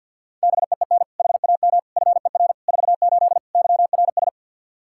Morse code, heard as a keyed tone, about 51 wpm, 700 Hz. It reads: BEEN HAM FER 48 YRS
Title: Practice QSO - Exchange 3 - More Info - 50wpm